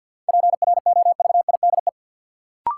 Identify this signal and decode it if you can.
Morse code, heard as a keyed tone, about 52 wpm, 700 Hz. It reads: PROVIDE
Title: Sentences from Top 1100 Words - New Word Drill Alt - 50wpm